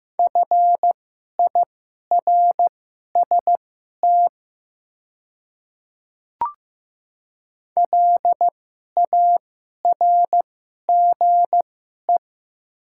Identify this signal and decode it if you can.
Morse code, heard as a keyed tone, about 15 wpm, spaced as for 11 wpm, 700 Hz. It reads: FIRST LARGE
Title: 2x Character Spacing - Top 200 Words - 15wpm